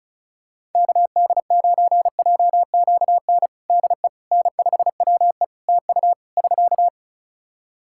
Morse code, heard as a keyed tone, about 36 wpm, 700 Hz. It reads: KD9JQN DE N5WE TU <SK>
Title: Practice QSO - Exchange 4 - Ending - 35wpm